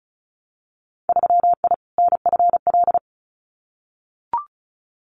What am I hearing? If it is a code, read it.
Morse code, heard as a keyed tone, about 34 wpm, 700 Hz. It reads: 3I NFL
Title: Winter Field Day Exchanges 35wpm